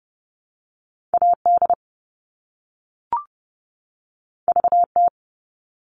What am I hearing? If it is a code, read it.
Morse code, heard as a keyed tone, about 30 wpm, 700 Hz. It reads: AD VT
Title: All Cut Numbers - 2-Digit 30wpm